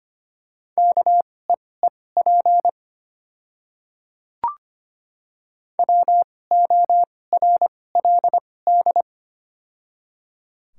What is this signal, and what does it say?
Morse code, heard as a keyed tone, about 25 wpm, spaced as for 18 wpm, 700 Hz. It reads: KEEP WORLD
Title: 2x Character Spacing - Top 500 Words - 25wpm